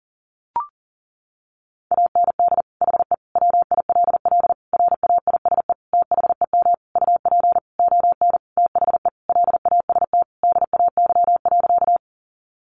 Morse code, heard as a keyed tone, about 40 wpm, 700 Hz. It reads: AND HE WILL RAISE THEM UP ON THE LAST DAY.
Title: Sentences from Top 1400 Words 40wpm